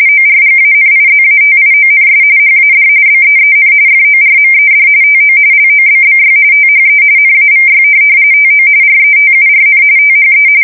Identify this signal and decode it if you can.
RTTY, about 45.45 baud. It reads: ORGANIZATIONS HAD BEEN FORMED TO LOBBY THE NAVY TO RETURN THE
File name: fsk170 / USS-New-Jersey